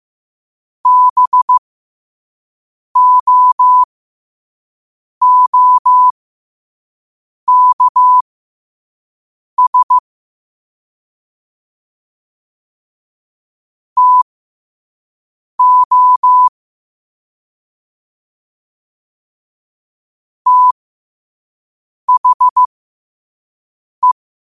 Morse code, heard as a keyed tone, about 15 wpm, spaced as for 5 wpm, 1000 Hz. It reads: BOOKS TO THE